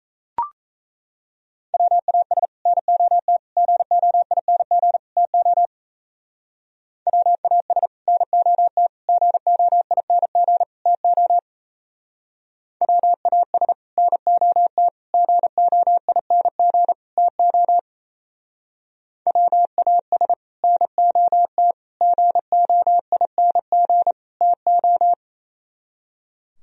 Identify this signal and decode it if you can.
Morse code, heard as a keyed tone, about 32 wpm, 700 Hz. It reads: WAS NOT GOING TO WAS NOT GOING TO WAS NOT GOING TO WAS NOT GOING TO
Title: Speed-Racing - Sets of 4 Words 28wpm